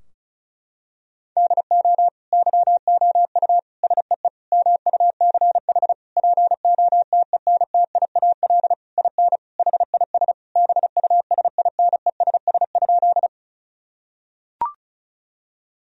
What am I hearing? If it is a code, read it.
Morse code, heard as a keyed tone, about 36 wpm, 700 Hz. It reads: DO YOU SEE MUCH POTENTIAL IN HIS BUSINESS?
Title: Sentences from Top 1000 Words - Review Part 4 - 35wpm